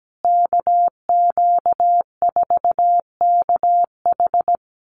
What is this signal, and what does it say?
Morse code, heard as a keyed tone, about 17 wpm, 700 Hz. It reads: KQ4KH
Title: USA Call Signs Repeated 2x 17wpm